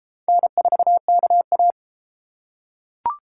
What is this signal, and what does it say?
Morse code, heard as a keyed tone, about 33 wpm, 700 Hz. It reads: N4KA